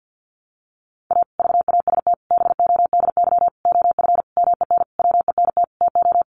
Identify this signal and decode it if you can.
Morse code, heard as a keyed tone, about 50 wpm, 700 Hz. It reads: A VAST BODY OF MEN WENT TO
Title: Sentences from Top 1300 Words 50wpm